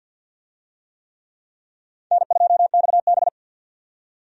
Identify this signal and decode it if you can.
Morse code, heard as a keyed tone, about 50 wpm, 700 Hz. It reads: N2XB
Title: USA Call Signs - 4 Characters - 50wpm